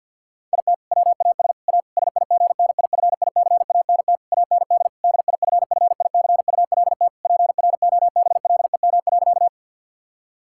Morse code, heard as a keyed tone, about 50 wpm, 700 Hz. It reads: IT WAS A SIGNIFICANT AND DIFFICULT PROBLEM.
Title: Sentences from Top 1000 Words - Review Part 6 - 50wpm